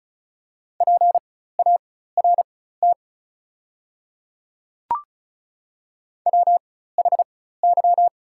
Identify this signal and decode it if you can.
Morse code, heard as a keyed tone, about 34 wpm, spaced as for 16 wpm, 700 Hz. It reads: PART WHY
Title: 4x Character Spacing - Top 200 Words - 35wpm